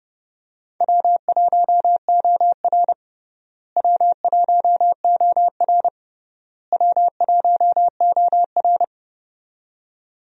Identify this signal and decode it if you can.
Morse code, heard as a keyed tone, about 32 wpm, 700 Hz. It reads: W1OR W1OR W1OR
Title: USA Call Signs - 4 Characters - Repeated 3x - 30wpm